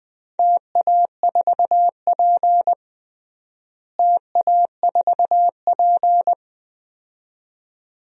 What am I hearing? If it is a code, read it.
Morse code, heard as a keyed tone, about 20 wpm, 700 Hz. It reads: TA4P TA4P
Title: DXCC-100 Call Signs - 4 Characters - Repeated 2x - 20wpm